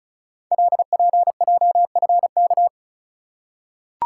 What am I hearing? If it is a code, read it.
Morse code, heard as a keyed tone, about 36 wpm, 700 Hz. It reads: LPJFK
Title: Code Groups - 5 Letters - 35wpm